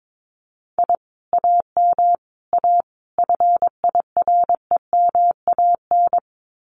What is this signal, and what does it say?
Morse code, heard as a keyed tone, about 22 wpm, 700 Hz. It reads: I AM A FIREMAN
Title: Common QSO Phrases Encore 22wpm